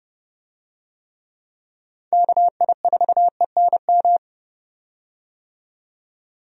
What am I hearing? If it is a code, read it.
Morse code, heard as a keyed tone, about 30 wpm, 700 Hz. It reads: KI4ENM
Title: USA Call Signs - 6 Characters - 30wpm